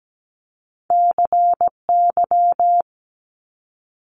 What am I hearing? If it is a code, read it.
Morse code, heard as a keyed tone, about 17 wpm, 700 Hz. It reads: CY
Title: Multi-letter Phonemes Quick Fire 17wpm